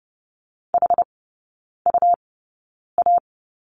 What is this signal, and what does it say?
Morse code, heard as a keyed tone, about 30 wpm, spaced as for 9 wpm, 700 Hz. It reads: HUA
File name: Lesson 17e - Letter u - Three Characters 30wpm